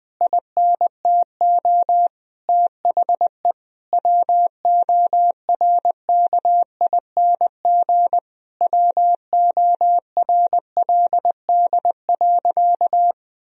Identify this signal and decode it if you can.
Morse code, heard as a keyed tone, about 20 wpm, 700 Hz. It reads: INTO THE WORKING WORLD.
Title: Sentences from Top 1000 Words - Review Part 4 - 20wpm